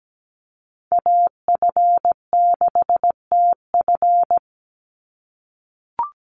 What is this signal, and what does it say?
Morse code, heard as a keyed tone, about 17 wpm, 700 Hz. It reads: AF6TF
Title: USA Call Signs 17wpm